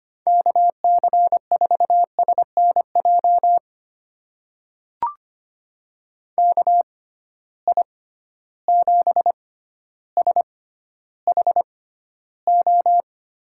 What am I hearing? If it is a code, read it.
Morse code, heard as a keyed tone, about 25 wpm, spaced as for 13 wpm, 700 Hz. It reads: KC4SNJ KI7SHO
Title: USA Call Signs - 6 Characters - 6x Character Spacing - 25wpm